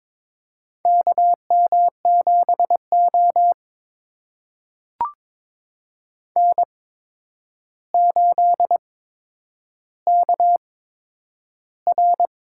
Morse code, heard as a keyed tone, about 22 wpm, spaced as for 9 wpm, 700 Hz. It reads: KM7O N8KR
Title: USA Call Signs - 4 Characters - 8x Character Spacing - 22wpm